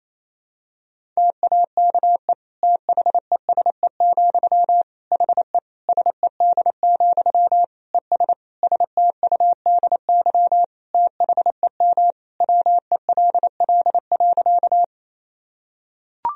Morse code, heard as a keyed tone, about 28 wpm, 700 Hz. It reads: TAKE THESE, HE SED, ES STUDY THEM WELL.
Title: Sentences from Top 300 Words and Abbreviations 28wpm